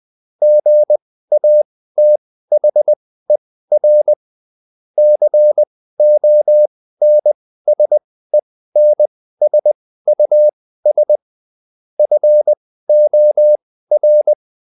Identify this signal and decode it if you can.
Morse code, heard as a keyed tone, about 20 wpm, spaced as for 14 wpm, 600 Hz. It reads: GATHER CONSENSUS FOR